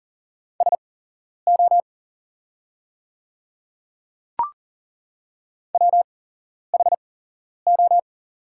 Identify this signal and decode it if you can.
Morse code, heard as a keyed tone, about 40 wpm, spaced as for 11 wpm, 690 Hz. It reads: SO WHO
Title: 8x Character Spacing - Top 100 Words - 40wpm